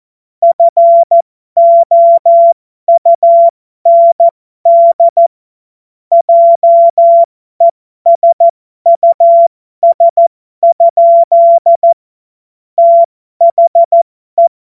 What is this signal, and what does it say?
Morse code, heard as a keyed tone, about 14 wpm, spaced as for 12 wpm, 680 Hz. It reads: FOUND JESUS? THE